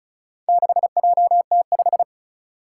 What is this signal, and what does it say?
Morse code, heard as a keyed tone, about 36 wpm, 700 Hz. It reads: 6JT5
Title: Four Challenging Letter-Number Combinations 35wpm